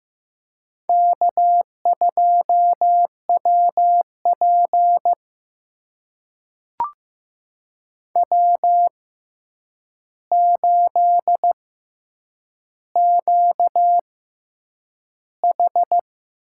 Morse code, heard as a keyed tone, about 15 wpm, spaced as for 8 wpm, 700 Hz. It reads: K2WP W8QH
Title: USA Call Signs - 4 Characters - 6x Character Spacing - 15wpm